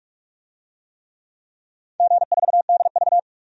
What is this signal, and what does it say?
Morse code, heard as a keyed tone, about 44 wpm, 700 Hz. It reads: G4DV
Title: DXCC-100 Call Signs - 4 Characters - 45wpm